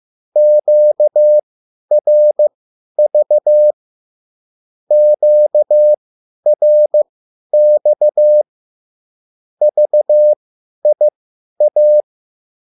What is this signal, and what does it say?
Morse code, heard as a keyed tone, about 15 wpm, spaced as for 10 wpm, 600 Hz. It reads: QRV QRX VIA
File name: QSL / CW lesson